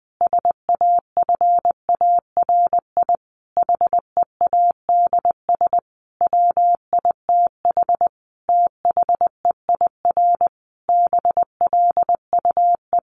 Morse code, heard as a keyed tone, about 20 wpm, 700 Hz. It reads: SAFARI HEADS WITH THEIR BLUE
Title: A Sound Of Thunder 20wpm